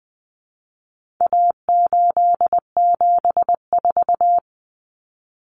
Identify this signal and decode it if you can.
Morse code, heard as a keyed tone, about 20 wpm, 700 Hz. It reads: A874
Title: Common Cut Numbers - 4-Digit 20wpm